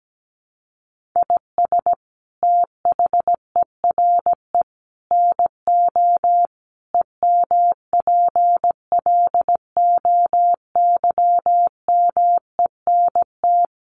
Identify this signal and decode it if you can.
Morse code, heard as a keyed tone, about 17 wpm, 700 Hz. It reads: IS THERE NO EMPLOYMENT
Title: Sentences from Top 1500 Words 17wpm